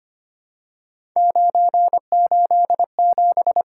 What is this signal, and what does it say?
Morse code, heard as a keyed tone, about 25 wpm, 700 Hz. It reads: 987
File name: Three Letter-Number Combinations - 25wpm